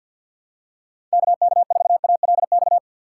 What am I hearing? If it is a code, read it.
Morse code, heard as a keyed tone, about 52 wpm, 700 Hz. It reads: KK4ALX
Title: USA Call Signs - 6 Characters - 3x Character Spacing - 50wpm